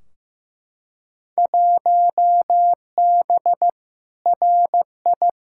Morse code, heard as a keyed tone, about 15 wpm, 700 Hz. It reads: 1B RI
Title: Field Day Exchanges Encore - 15wpm